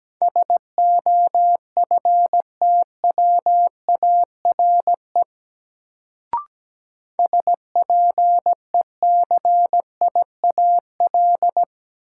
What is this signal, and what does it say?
Morse code, heard as a keyed tone, about 17 wpm, 700 Hz. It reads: SOFTWARE SPECIAL
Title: Sentences from Top 1000 Words - Review Part 3 - New Word Drill - 17wpm